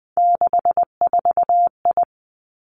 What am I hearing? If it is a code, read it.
Morse code, heard as a keyed tone, about 20 wpm, 700 Hz. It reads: 64I
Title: Three Challenging Letter-Number Combinations 20wpm